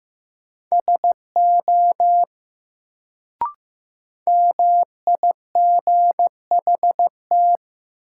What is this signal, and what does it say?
Morse code, heard as a keyed tone, about 15 wpm, 700 Hz. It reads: SO MIGHT
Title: Sets of 2 Words - New Word Drill - 15wpm